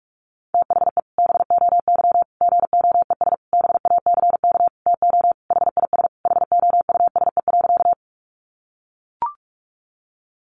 Morse code, heard as a keyed tone, about 44 wpm, 700 Hz. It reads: THE BOY GOES BACK TO HIS HOUSE.
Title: Additional Easy Phrases 45wpm